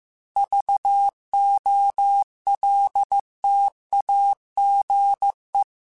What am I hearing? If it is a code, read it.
Morse code, heard as a keyed tone, about 15 wpm, 800 Hz. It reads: VOLTAGE